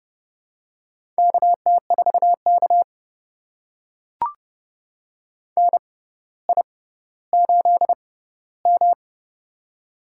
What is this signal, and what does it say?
Morse code, heard as a keyed tone, about 31 wpm, spaced as for 15 wpm, 700 Hz. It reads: KT4K NI8M
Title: USA Call Signs - 4 Characters - 6x Character Spacing - 30wpm